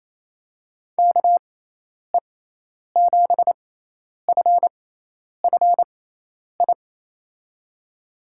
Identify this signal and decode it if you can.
Morse code, heard as a keyed tone, about 28 wpm, spaced as for 10 wpm, 700 Hz. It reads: KE7FFI
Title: USA Call Signs - 6x Character Spacing - 28wpm